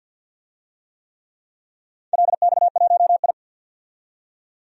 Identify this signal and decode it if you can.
Morse code, heard as a keyed tone, about 48 wpm, 700 Hz. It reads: LX1I
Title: DXCC-100 Call Signs - 4 Characters - 50wpm